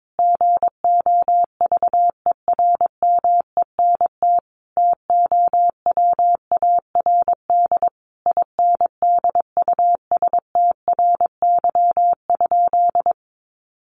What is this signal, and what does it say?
Morse code, heard as a keyed tone, about 22 wpm, 700 Hz. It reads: GOVERMENT TOWARD INDUSTRY?
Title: Sentences from Top 1300 Words 22wpm